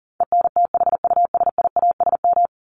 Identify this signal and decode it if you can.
Morse code, heard as a keyed tone, about 40 wpm, 700 Hz. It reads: ENTHUSIASM
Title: Top 1751-2000 Words 40wpm